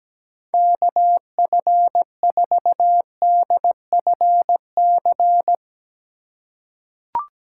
Morse code, heard as a keyed tone, about 17 wpm, 700 Hz. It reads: KF4DFC
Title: USA Call Signs - 6 Characters - 3x Character Spacing - 17wpm